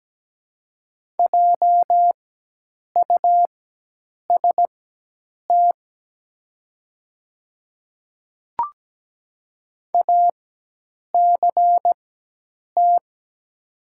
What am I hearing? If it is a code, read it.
Morse code, heard as a keyed tone, about 17 wpm, spaced as for 8 wpm, 700 Hz. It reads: JUST ACT
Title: 4x Character Spacing - Top 200 Words - 17wpm